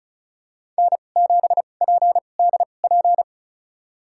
Morse code, heard as a keyed tone, about 36 wpm, spaced as for 25 wpm, 700 Hz. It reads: N7PDP
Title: USA Call Signs - 2x Character Spacing - 35wpm